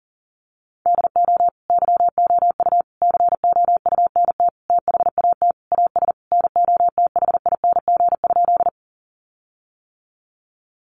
Morse code, heard as a keyed tone, about 40 wpm, 700 Hz. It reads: DO YOU COUNT THAT AS NOTHING?
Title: Sentences from Top 1500 Words 40wpm